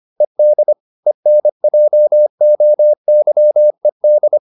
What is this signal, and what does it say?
Morse code, heard as a keyed tone, about 25 wpm, 600 Hz. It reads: ED ENJOYED